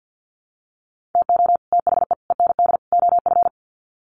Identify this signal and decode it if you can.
Morse code, heard as a keyed tone, about 50 wpm, 700 Hz. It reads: TO THE END OF
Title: Sets of 4 Words 50wpm